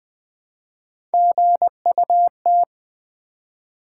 Morse code, heard as a keyed tone, about 20 wpm, 700 Hz. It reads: GUT
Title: Build A Word - 1 Letter At A Time - 3-Letter Words - 20wpm